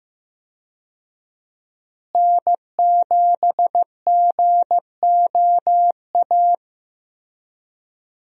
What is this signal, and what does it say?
Morse code, heard as a keyed tone, about 15 wpm, 700 Hz. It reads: N7GOA
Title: USA Call Signs - 5 Characters - Repeated 2x - 15wpm